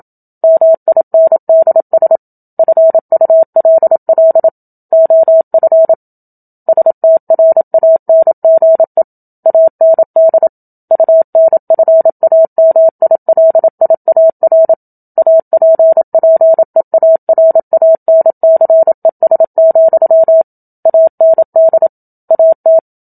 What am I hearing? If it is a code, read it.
Morse code, heard as a keyed tone, about 27 wpm, 650 Hz. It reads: MINDS FULL OF STRANGE AND UNFAMILIAR APPEARANCES, AND AT
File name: Book Two / 01-Chapter 1 - 27 WPM